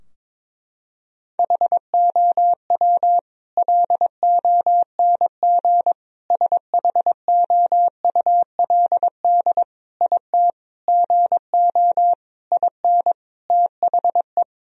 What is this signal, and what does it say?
Morse code, heard as a keyed tone, about 22 wpm, 700 Hz. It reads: HOW LONG SHOULD IT GO IN THE
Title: Sentences from Top 1000 Words - Review Part 4 - 22wpm